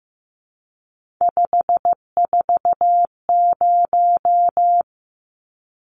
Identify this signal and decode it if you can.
Morse code, heard as a keyed tone, about 15 wpm, 700 Hz. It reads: 540
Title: Numbers - 3-Digit 15wpm